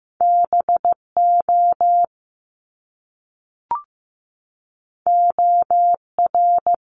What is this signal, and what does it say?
Morse code, heard as a keyed tone, about 15 wpm, 700 Hz. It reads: BO OR
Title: Top Two Letter Combinations in Words 15wpm